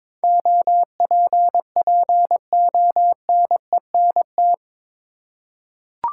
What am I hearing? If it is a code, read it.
Morse code, heard as a keyed tone, about 21 wpm, 700 Hz. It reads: OPPONENT
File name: practice / 22wpm / Eight Letter Words - 22wpm